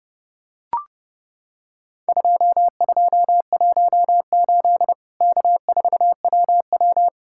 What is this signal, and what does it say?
Morse code, heard as a keyed tone, about 30 wpm, 700 Hz. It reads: 2218 K4WW
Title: Worked All Europe DX Contest - QTC Exchange - 30wpm